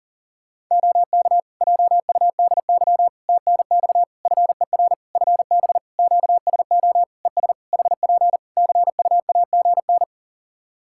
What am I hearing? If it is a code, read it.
Morse code, heard as a keyed tone, about 40 wpm, 700 Hz. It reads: OK JUDY TNX FER FB QSO ES HP CUAGN
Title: Practice QSO - Exchange 4 - Ending - 40wpm